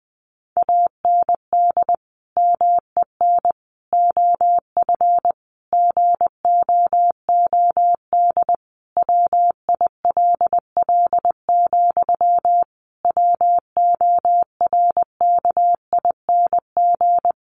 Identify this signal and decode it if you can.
Morse code, heard as a keyed tone, about 20 wpm, 700 Hz. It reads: AND MEN OF GOOD WILL, WORKING
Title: Sentences from Top 1400 Words 20wpm